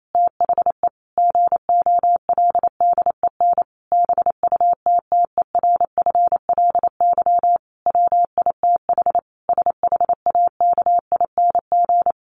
Morse code, heard as a keyed tone, about 28 wpm, 700 Hz. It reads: THE GOLDEN BUTTERFLY WITH SHAKING
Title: A Sound Of Thunder 28wpm